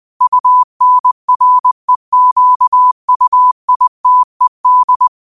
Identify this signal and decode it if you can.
Morse code, heard as a keyed tone, about 20 wpm, 1000 Hz. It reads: UNREQUITED
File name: qotdfiles / 20wpm